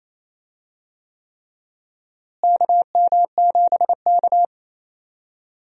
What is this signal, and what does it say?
Morse code, heard as a keyed tone, about 28 wpm, 700 Hz. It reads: KM7K
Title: USA Call Signs - 4 Characters - Repeated 3x - 28wpm